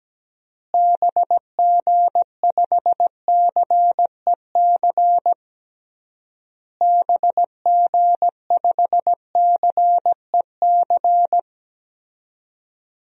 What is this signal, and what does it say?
Morse code, heard as a keyed tone, about 17 wpm, 700 Hz. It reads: BG5CEC BG5CEC
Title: DXCC-100 Call Signs - 6 Characters - Repeated 3x - 17wpm